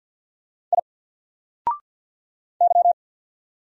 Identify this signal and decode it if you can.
Morse code, heard as a keyed tone, about 50 wpm, 700 Hz. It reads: I Y
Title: Single Letter-Number Alt 50wpm